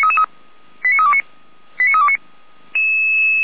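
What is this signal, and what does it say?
Five-tone selective call (ZVEI), heard as quick runs of five tones, tones of about 80 ms. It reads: 89329 89329
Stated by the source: Five-tone ZVEI selective call received on an SDR, from the feuerwehr-orlamuende.org fire brigade audio server.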